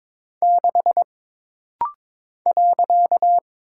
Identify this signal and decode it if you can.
Morse code, heard as a keyed tone, about 22 wpm, 700 Hz. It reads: 6 .
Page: This rapid-fire timing is geared to push your ICR (Instant Character Recognition) skills.